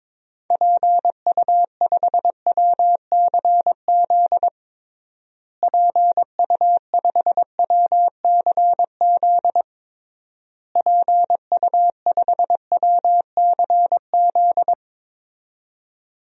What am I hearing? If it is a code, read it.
Morse code, heard as a keyed tone, about 23 wpm, 700 Hz. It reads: PU5WCZ PU5WCZ PU5WCZ
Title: DXCC-100 Call Signs - 6 Characters - Repeated 3x - 22wpm